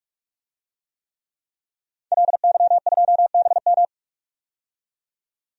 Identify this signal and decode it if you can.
Morse code, heard as a keyed tone, about 47 wpm, 700 Hz. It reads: LY2BM
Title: DXCC-100 Call Signs - 5 Characters - 45wpm